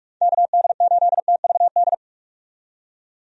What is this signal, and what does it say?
Morse code, heard as a keyed tone, about 44 wpm, 700 Hz. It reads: KD8TVD
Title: USA Call Signs - 6 Characters - 45wpm